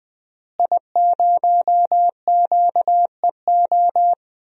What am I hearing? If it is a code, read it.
Morse code, heard as a keyed tone, about 20 wpm, 700 Hz. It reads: I0QEO
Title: DXCC-100 Call Signs - 5 Characters - Repeated 3x - 20wpm